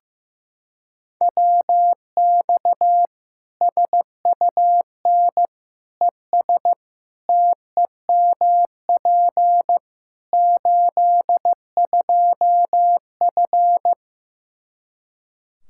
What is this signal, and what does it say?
Morse code, heard as a keyed tone, about 15 wpm, 700 Hz. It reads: WX SUN ES TEMP 82F
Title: K1USN SST QSO Practice 15wpm